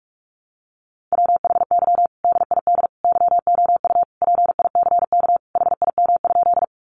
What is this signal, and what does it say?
Morse code, heard as a keyed tone, about 44 wpm, 700 Hz. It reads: WHY DID YOU PICK HIM?
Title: Sentences from Top 1000 Words 45wpm